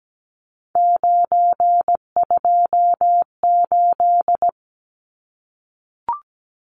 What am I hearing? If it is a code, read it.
Morse code, heard as a keyed tone, about 17 wpm, 700 Hz. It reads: 928
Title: Numbers - 3-Digit 17wpm